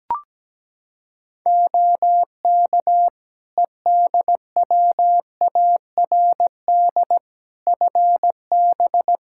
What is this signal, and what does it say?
Morse code, heard as a keyed tone, about 17 wpm, 700 Hz. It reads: OK EDWARD FB
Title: Practice QSO - Exchange 2 - Info - 17wpm